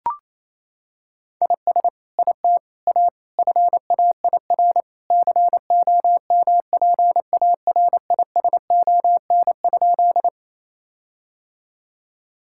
Morse code, heard as a keyed tone, about 28 wpm, 700 Hz. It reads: IS IT A FAIR COMPARISON?
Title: Sentences from Top 1600 Words 28wpm